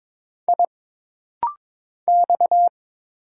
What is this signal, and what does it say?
Morse code, heard as a keyed tone, about 22 wpm, 700 Hz. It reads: I X
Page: This rapid-fire timing is geared to push your ICR (Instant Character Recognition) skills.